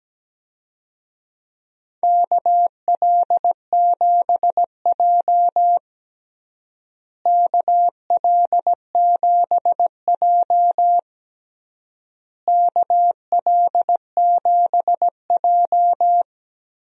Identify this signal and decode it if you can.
Morse code, heard as a keyed tone, about 17 wpm, 700 Hz. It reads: KL7J KL7J KL7J
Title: DXCC-100 Call Signs - 4 Characters - Repeated 3x - 17wpm